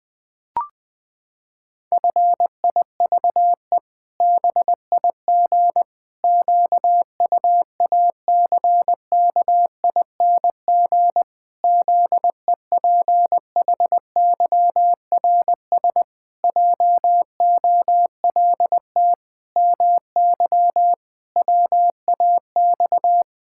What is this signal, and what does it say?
Morse code, heard as a keyed tone, about 20 wpm, 700 Hz. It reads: FIVE BIG QUACKING ZEPHYRS JOLT MY WAX
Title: Pangram Challenge v2 - 20wpm